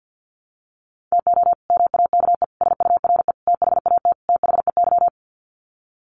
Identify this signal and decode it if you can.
Morse code, heard as a keyed tone, about 50 wpm, 700 Hz. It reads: TO MAKE SURE THAT THEY
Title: Sets of 5 Words Encore2 50wpm